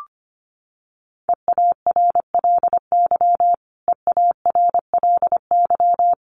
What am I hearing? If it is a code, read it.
Morse code, heard as a keyed tone, about 25 wpm, 700 Hz. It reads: EARLY EARLY
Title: Top 500 Words Repeated 25wpm